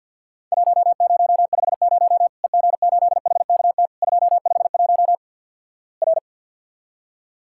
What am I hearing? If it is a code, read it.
Morse code, heard as a keyed tone, about 50 wpm, 700 Hz. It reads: 1050 EG8HKT 251 R
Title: Worked All Europe DX Contest - QTC Exchange - 50wpm